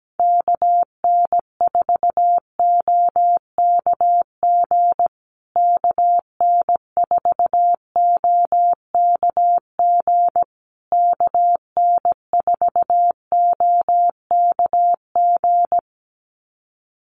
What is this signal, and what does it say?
Morse code, heard as a keyed tone, about 17 wpm, 700 Hz. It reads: KN4OKG KN4OKG KN4OKG
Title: USA Call Signs Repeated 3x 17wpm